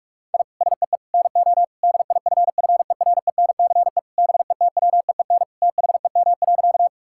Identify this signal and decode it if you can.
Morse code, heard as a keyed tone, about 46 wpm, 700 Hz. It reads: I SEE NO DIFFERENCE BETWEEN THEM.
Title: Sentences from Top 1000 Words - Review Part 5 - 45wpm